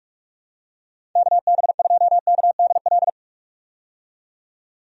Morse code, heard as a keyed tone, about 46 wpm, 700 Hz. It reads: KB2KDL
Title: USA Call Signs - 6 Characters - 45wpm